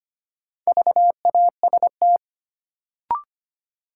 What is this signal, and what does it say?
Morse code, heard as a keyed tone, about 25 wpm, 700 Hz. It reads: VAST
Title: Sets of 4 Words - New Word Drill - 25wpm